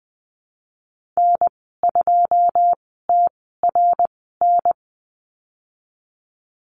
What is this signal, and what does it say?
Morse code, heard as a keyed tone, about 20 wpm, spaced as for 14 wpm, 700 Hz. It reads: N2TRN
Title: 088 Lesson 15g - Number 5 - Call Signs 20wpm